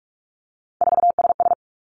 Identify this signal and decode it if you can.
Morse code, heard as a keyed tone, about 46 wpm, 700 Hz. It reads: 4SS
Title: Three Challenging Letter-Number Combinations 45wpm